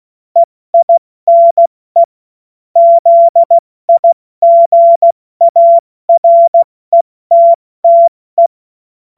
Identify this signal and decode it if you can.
Morse code, heard as a keyed tone, about 16 wpm, 680 Hz. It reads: EINE ZIGARETTE
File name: Morsekurs / Witze / Witz 27. Juli 2021, 70 F80